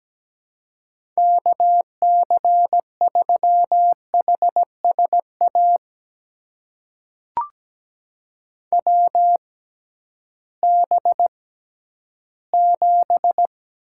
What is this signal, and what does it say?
Morse code, heard as a keyed tone, about 17 wpm, spaced as for 11 wpm, 700 Hz. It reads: KC3HSA WB7
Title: USA Call Signs - 6 Characters - 6x Character Spacing - 17wpm